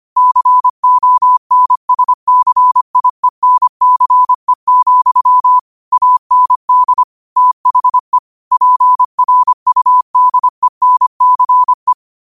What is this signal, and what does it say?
Morse code, heard as a keyed tone, about 25 wpm, 1000 Hz. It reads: CONSCIENCE, AND THE PRUDENCE
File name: qotdfiles / 25wpm